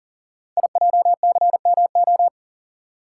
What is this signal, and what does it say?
Morse code, heard as a keyed tone, about 42 wpm, 700 Hz. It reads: IJCMO
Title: Code Groups - 5 Letters - 40wpm